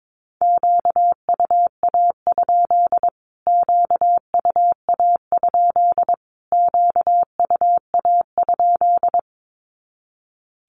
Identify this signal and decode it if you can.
Morse code, heard as a keyed tone, about 22 wpm, 700 Hz. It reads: QUA? QUA? QUA?
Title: Q-Codes Repeated 3x 22wpm